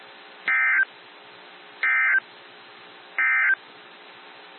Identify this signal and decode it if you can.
Emergency Alert System (SAME) tones, heard as harsh buzzing bursts. It is heard from 0.4 to 3.6 s.
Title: NOAA Weather Radio - WXJ73